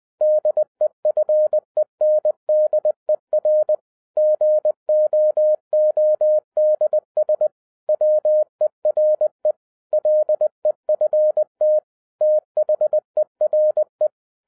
Morse code, heard as a keyed tone, about 20 wpm, 610 Hz. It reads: DEFENDER GOODS WERE LEFT THERE
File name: 2011-10-mp3 / n03-TARO JR0QWW 20wpm